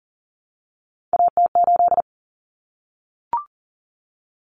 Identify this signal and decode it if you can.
Morse code, heard as a keyed tone, about 40 wpm, 700 Hz. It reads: AT8
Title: Common Cut Numbers - 3-Digit 40wpm